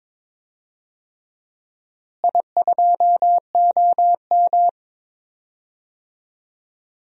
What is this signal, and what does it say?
Morse code, heard as a keyed tone, about 22 wpm, 700 Hz. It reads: I2OM
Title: DXCC-100 Call Signs - 4 Characters - Repeated 3x - 22wpm